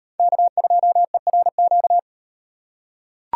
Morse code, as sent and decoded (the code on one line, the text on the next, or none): -.- ..--- . .-. --.-
K2ERQ